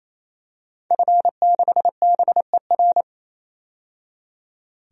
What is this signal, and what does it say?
Morse code, heard as a keyed tone, about 28 wpm, 700 Hz. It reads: F6BER
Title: DXCC-100 Call Signs - 5 Characters - 28wpm